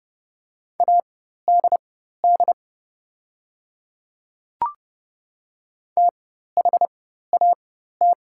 Morse code, heard as a keyed tone, about 30 wpm, spaced as for 14 wpm, 700 Hz. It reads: ADD THAT
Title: 4x Character Spacing - Top 200 Words - 30wpm